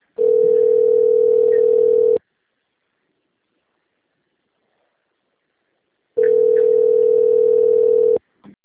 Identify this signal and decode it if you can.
Ringback tone (North America), 440 + 480 Hz.